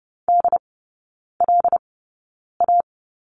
Morse code, heard as a keyed tone, about 30 wpm, spaced as for 9 wpm, 700 Hz. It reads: DLA
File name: Lesson 16e - Letter c - Three Characters 30wpm